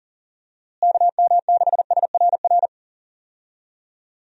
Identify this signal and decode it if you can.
Morse code, heard as a keyed tone, about 40 wpm, 700 Hz. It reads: KM6SRR
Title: USA Call Signs - 6 Characters - Repeated 3x - 40wpm